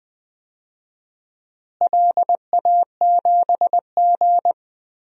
Morse code, heard as a keyed tone, about 20 wpm, 700 Hz. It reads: LA7G
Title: DXCC-100 Call Signs - 4 Characters - Repeated 2x - 20wpm